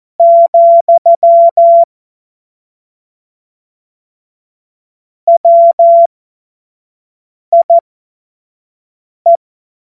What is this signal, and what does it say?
Morse code, heard as a keyed tone, about 14 wpm, spaced as for 5 wpm, 680 Hz. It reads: , WIE
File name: Morsekurs / Witze / Witz 10. Mai 2022 25 F70